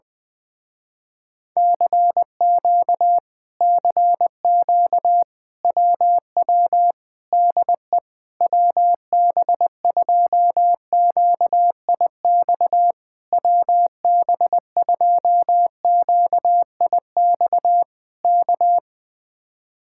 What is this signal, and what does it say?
Morse code, heard as a keyed tone, about 20 wpm, 700 Hz. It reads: CQ CQ WW DE WB2QIX WB2QIX K
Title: Speed-Racing Common QSO Phrases - 17wpm